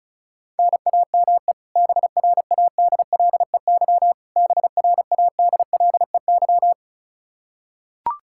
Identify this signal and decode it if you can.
Morse code, heard as a keyed tone, about 36 wpm, 700 Hz. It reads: NAME BRADLEY BRADLEY
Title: Practice QSO - Exchange 1 - Beginning - 35wpm